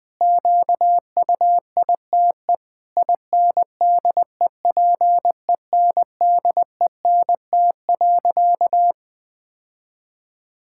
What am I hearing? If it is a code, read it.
Morse code, heard as a keyed tone, about 20 wpm, 700 Hz. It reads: QUITE INDEPENDENT.
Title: Sentences from Top 1000 Words - Review Part 6 - 20wpm